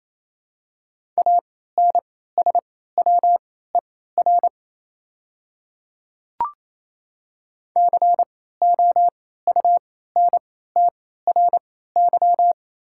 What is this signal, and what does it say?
Morse code, heard as a keyed tone, about 27 wpm, spaced as for 16 wpm, 700 Hz. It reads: ANSWER COUNTRY
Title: 3x Character Spacing - Top 300 Words - 28wpm